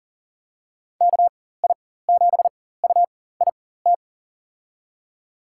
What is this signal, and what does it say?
Morse code, heard as a keyed tone, about 40 wpm, spaced as for 19 wpm, 700 Hz. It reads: KI7UIT
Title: USA Call Signs - 6 Characters - 4x Character Spacing - 40wpm